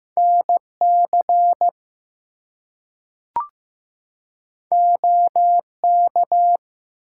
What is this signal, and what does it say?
Morse code, heard as a keyed tone, about 15 wpm, 700 Hz. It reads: NC OK
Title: US States and Canadian Provinces 15wpm